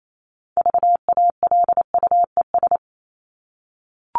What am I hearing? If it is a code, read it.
Morse code, heard as a keyed tone, about 28 wpm, 700 Hz. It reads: VALUES
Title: ES:S Suffix - Top 100 Words with Base Words 28wpm